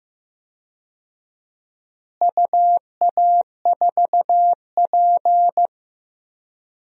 Morse code, heard as a keyed tone, about 15 wpm, 700 Hz. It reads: UA4P
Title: DXCC-100 Call Signs - 4 Characters - Repeated 2x - 15wpm